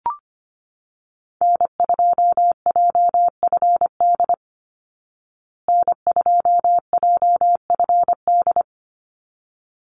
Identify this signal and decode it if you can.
Morse code, heard as a keyed tone, about 25 wpm, 700 Hz. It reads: N2JFD N2JFD
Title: Call Signs Repeated 3x - 25wpm